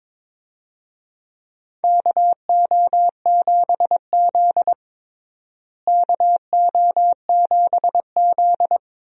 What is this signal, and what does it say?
Morse code, heard as a keyed tone, about 21 wpm, 700 Hz. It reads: KO7Z KO7Z
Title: USA Call Signs - 4 Characters - Repeated 2x - 22wpm